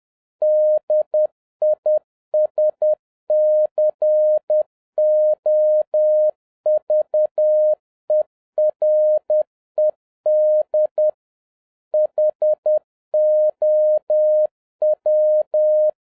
Morse code, heard as a keyed tone, about 10 wpm, 610 Hz. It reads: DISCOVERED HOW